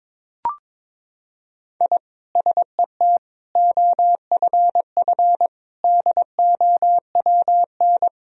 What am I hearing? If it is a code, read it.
Morse code, heard as a keyed tone, about 22 wpm, 700 Hz. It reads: I SET OFF DOWN
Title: Sentences from Top 1000 Words - Review Part 1 - 22wpm